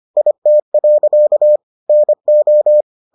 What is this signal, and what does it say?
Morse code, heard as a keyed tone, about 25 wpm, 600 Hz. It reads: IT. NO